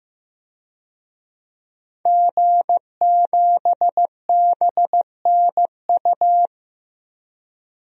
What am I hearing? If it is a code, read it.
Morse code, heard as a keyed tone, about 15 wpm, 700 Hz. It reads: G7BNU
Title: DXCC-100 Call Signs - 5 Characters - Repeated 3x - 15wpm